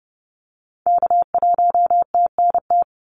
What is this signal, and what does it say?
Morse code, heard as a keyed tone, about 30 wpm, 700 Hz. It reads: K1TNT
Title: NA SOTA Activator Callsigns 30wpm